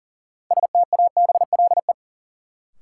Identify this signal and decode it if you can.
Morse code, heard as a keyed tone, about 40 wpm, 700 Hz. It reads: STABLE
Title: Sentences from Top 1500 Words - New Word Drill - 40wpm